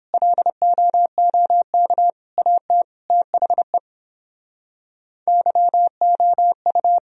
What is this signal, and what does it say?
Morse code, heard as a keyed tone, about 30 wpm, 700 Hz. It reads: LOOK AT THE YOU